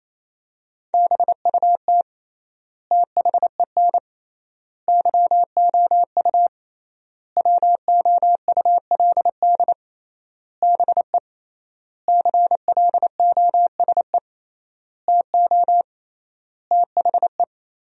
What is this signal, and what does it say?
Morse code, heard as a keyed tone, about 27 wpm, 700 Hz. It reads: BUT THEN YOU WOULD BE CLOSE TO THE
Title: 3x Word-Spacing - Sentences from Top 300 Words - 28wpm